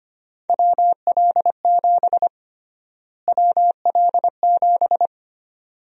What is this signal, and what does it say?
Morse code, heard as a keyed tone, about 25 wpm, 700 Hz. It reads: WL7 WL7
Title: USA Call Sign Prefixes - Repeated 2x - 25wpm